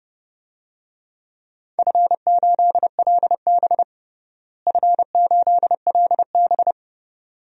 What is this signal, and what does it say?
Morse code, heard as a keyed tone, about 30 wpm, 700 Hz. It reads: F8LB F8LB
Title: DXCC-100 Call Signs - 4 Characters - Repeated 2x - 30wpm